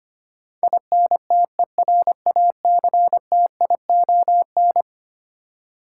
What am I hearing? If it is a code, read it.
Morse code, heard as a keyed tone, about 25 wpm, 700 Hz. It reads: INTERACTION
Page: morse-code-ninja-practice-25wpm